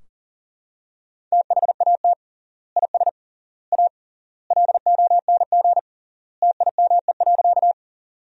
Morse code, heard as a keyed tone, about 40 wpm, 700 Hz. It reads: THAT IS A LONG TIME.
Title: 3x Word-Spacing - Sentences from Top 100 Words - 40wpm